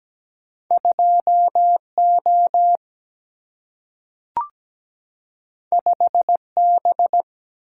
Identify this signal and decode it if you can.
Morse code, heard as a keyed tone, about 17 wpm, 700 Hz. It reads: 2O 5B
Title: Single-Letter Single-Number 17wpm